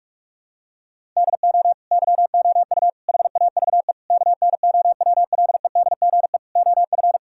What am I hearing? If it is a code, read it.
Morse code, heard as a keyed tone, about 45 wpm, 700 Hz. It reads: DO YOU HAVE KNOWLEDGE OF
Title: Sentences from Top 1000 Words - Review Part 6 - 45wpm